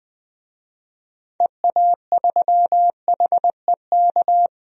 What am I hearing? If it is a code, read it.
Morse code, heard as a keyed tone, about 20 wpm, 700 Hz. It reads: EA3HEK
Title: DXCC-100 Call Signs - 6 Characters - 20wpm